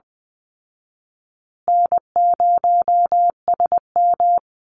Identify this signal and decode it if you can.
Morse code, heard as a keyed tone, about 20 wpm, 700 Hz. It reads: N0SM
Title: Easy Call Signs Repeated 3x - 20wpm